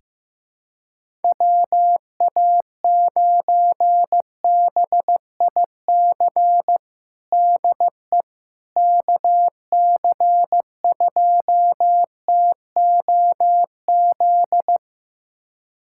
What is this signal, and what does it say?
Morse code, heard as a keyed tone, about 15 wpm, 700 Hz. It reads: WA9BIC DE KC2TOZ
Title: USA Callsign DE Callsign 15wpm